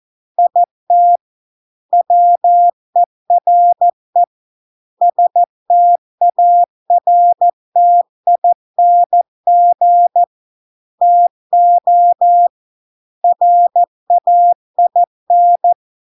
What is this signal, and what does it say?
Morse code, heard as a keyed tone, about 14 wpm, 700 Hz. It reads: IT WERE STARTING TO RAIN